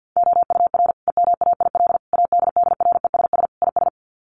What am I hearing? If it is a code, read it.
Morse code, heard as a keyed tone, about 50 wpm, 700 Hz. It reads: OUR EMAIL ADDRESS IS
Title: Sets of 5 Words Encore2 50wpm